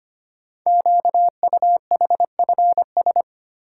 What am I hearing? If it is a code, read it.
Morse code, heard as a keyed tone, about 25 wpm, 700 Hz. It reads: QUHFS
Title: Code Groups - 5 Letters - 25wpm